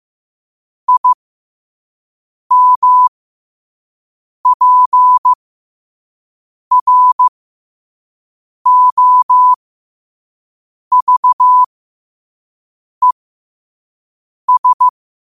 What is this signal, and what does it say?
Morse code, heard as a keyed tone, about 15 wpm, spaced as for 5 wpm, 1000 Hz. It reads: IMPROVES
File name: qotdfiles / 5wpm